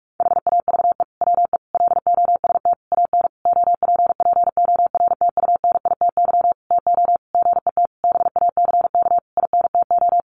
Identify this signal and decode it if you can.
Morse code, heard as a keyed tone, about 44 wpm, 700 Hz. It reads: HAVE WE LOST AN OPPORTUNITY TO GET BACK INTO
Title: Sentences from Top 1400 Words 45wpm